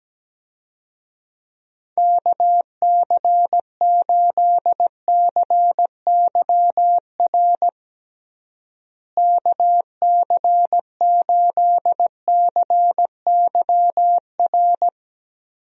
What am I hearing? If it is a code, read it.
Morse code, heard as a keyed tone, about 17 wpm, 700 Hz. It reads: KC8CYR KC8CYR
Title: USA Call Signs - 6 Characters - Repeated 3x - 17wpm